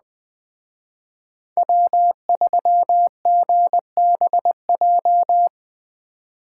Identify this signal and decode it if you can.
Morse code, heard as a keyed tone, about 21 wpm, 700 Hz. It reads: W3GBJ